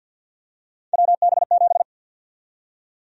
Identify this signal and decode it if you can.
Morse code, heard as a keyed tone, about 51 wpm, 700 Hz. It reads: WB7
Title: USA Call Sign Prefixes - 50wpm